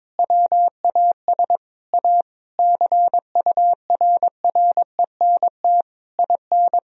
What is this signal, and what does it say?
Morse code, heard as a keyed tone, about 22 wpm, 700 Hz. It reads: WAS A CURRENT IN
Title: Sentences from Top 1000 Words - Review Part 4 - 22wpm